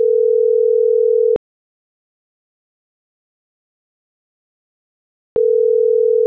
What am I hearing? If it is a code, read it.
Ringback tone (North America), 440 + 480 Hz.